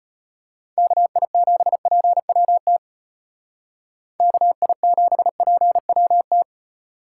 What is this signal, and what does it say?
Morse code, heard as a keyed tone, about 34 wpm, 700 Hz. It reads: KI7PWT KI7PWT